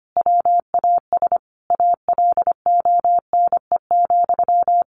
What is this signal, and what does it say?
Morse code, heard as a keyed tone, about 25 wpm, 700 Hz. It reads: WAS ALONE,
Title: Sentences from Top 1100 Words 25wpm